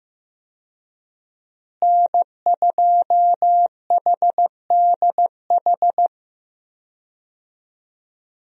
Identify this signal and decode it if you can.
Morse code, heard as a keyed tone, about 15 wpm, 700 Hz. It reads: N2HDH
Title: USA Call Signs - 5 Characters - Repeated 2x - 15wpm